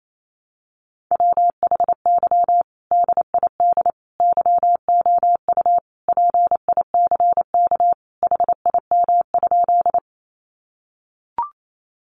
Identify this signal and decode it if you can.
Morse code, heard as a keyed tone, about 28 wpm, 700 Hz. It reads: WHY DID YOU PICK HIM?
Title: Sentences from Top 1000 Words 28wpm